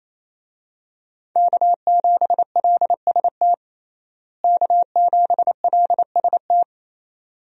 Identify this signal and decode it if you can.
Morse code, heard as a keyed tone, about 28 wpm, 700 Hz. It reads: K7LST K7LST
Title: USA Call Signs - 5 Characters - Repeated 2x - 28wpm